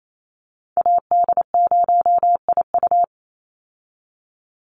Morse code, heard as a keyed tone, about 28 wpm, 700 Hz. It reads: AD0IU
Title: Call Signs - 28wpm